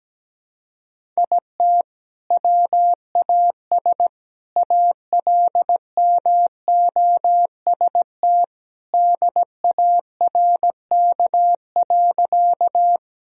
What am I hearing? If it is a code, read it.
Morse code, heard as a keyed tone, about 17 wpm, 700 Hz. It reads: IT WAS ALMOST DARK.
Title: Sentences from Top 1000 Words - Review Part 4 - 17wpm